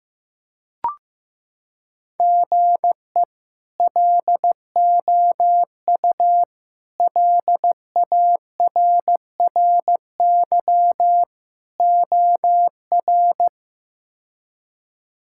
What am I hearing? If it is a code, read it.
Morse code, heard as a keyed tone, about 15 wpm, 700 Hz. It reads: GE LOU LARRY OR
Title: K1USN SST QSO Practice - 15wpm